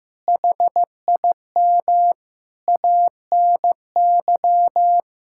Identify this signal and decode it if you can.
Morse code, heard as a keyed tone, about 15 wpm, 700 Hz. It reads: HIM ANY
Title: Sentences from Top 1000 Words - Review Part 6 - 15wpm